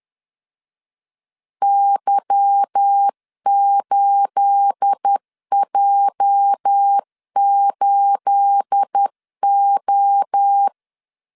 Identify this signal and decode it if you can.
Morse code, heard as a keyed tone, about 10 wpm, 800 Hz. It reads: Y8J8O
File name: LEZIONI / 9 28 10wpm 2842